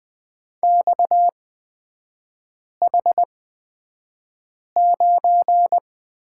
Morse code, heard as a keyed tone, about 20 wpm, spaced as for 5 wpm, 700 Hz. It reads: XH9
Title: 250 Lesson 40e - BK - Three Characters 20wpm